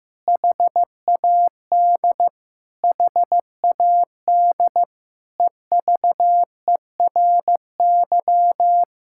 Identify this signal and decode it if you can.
Morse code, heard as a keyed tone, about 15 wpm, 700 Hz. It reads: HAD HAD EVERY
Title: Sentences from Top 1000 Words - Review Part 6 - 15wpm